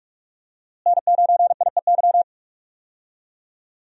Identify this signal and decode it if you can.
Morse code, heard as a keyed tone, about 43 wpm, 700 Hz. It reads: N9IEY
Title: USA Call Signs - 5 Characters - Repeated 2x - 45wpm